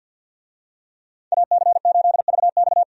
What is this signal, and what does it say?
Morse code, heard as a keyed tone, about 50 wpm, 700 Hz. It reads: AC8VX
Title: USA Call Signs - 5 Characters - 50wpm